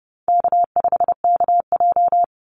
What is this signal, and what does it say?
Morse code, heard as a keyed tone, about 30 wpm, 700 Hz. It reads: K5KJ
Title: Call Signs - 30wpm